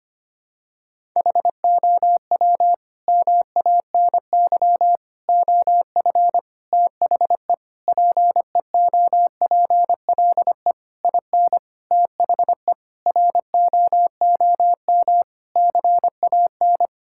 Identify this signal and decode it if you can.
Morse code, heard as a keyed tone, about 25 wpm, 700 Hz. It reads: HOW MANY OF THE PEOPLE IN THE ROOM CAN
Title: Sentences from Top 1000 Words - Review Part 6 - 25wpm